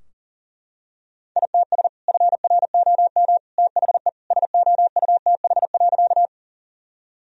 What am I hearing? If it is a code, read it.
Morse code, heard as a keyed tone, about 40 wpm, 700 Hz. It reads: ITS FROM THE SOUTH.
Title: Sentences from Top 1000 Words - Review Part 2 - 40wpm